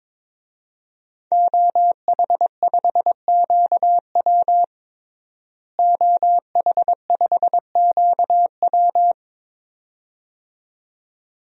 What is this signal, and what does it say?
Morse code, heard as a keyed tone, about 22 wpm, 700 Hz. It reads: OH5QW OH5QW
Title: DXCC-100 Call Signs - 5 Characters - Repeated 2x - 22wpm